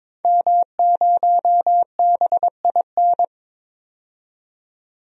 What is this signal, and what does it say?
Morse code, heard as a keyed tone, about 22 wpm, 700 Hz. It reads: M0BIN
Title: DXCC-100 Call Signs - 5 Characters - 22wpm